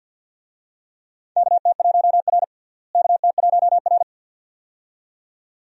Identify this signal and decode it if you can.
Morse code, heard as a keyed tone, about 50 wpm, 700 Hz. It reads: KT1R KT1R
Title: USA Call Signs - 4 Characters - Repeated 2x - 50wpm